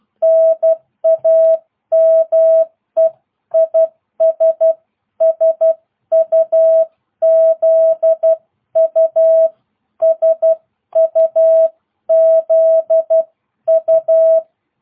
Morse code, heard as a keyed tone, about 12 wpm, 660 Hz. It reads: NAME IS SUZU SUZU